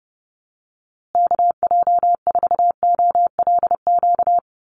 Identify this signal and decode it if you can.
Morse code, heard as a keyed tone, about 30 wpm, 700 Hz. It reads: KJ4OLQ
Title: USA Call Signs Repeated 2x 30wpm